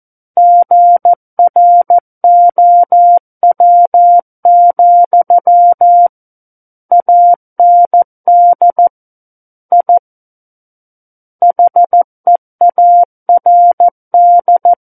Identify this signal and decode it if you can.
Morse code, heard as a keyed tone, about 14 wpm, 700 Hz. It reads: GROW, AND I HEARD
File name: Book One / 05-Chapter 5 - 14 WPM